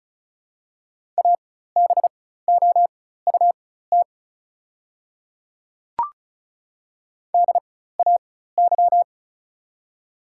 Morse code, heard as a keyed tone, about 36 wpm, spaced as for 16 wpm, 700 Hz. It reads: ABOUT DAY
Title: 4x Character Spacing - Top 200 Words - 35wpm